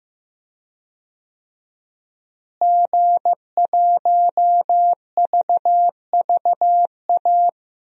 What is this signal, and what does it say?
Morse code, heard as a keyed tone, about 15 wpm, 700 Hz. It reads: G1VVA
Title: DXCC-100 Call Signs - 5 Characters - Repeated 2x - 15wpm